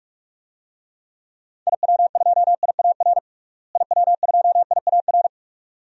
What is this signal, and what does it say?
Morse code, heard as a keyed tone, about 46 wpm, 700 Hz. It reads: IW2IAR IW2IAR
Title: DXCC-100 Call Signs - 6 Characters - Repeated 2x - 45wpm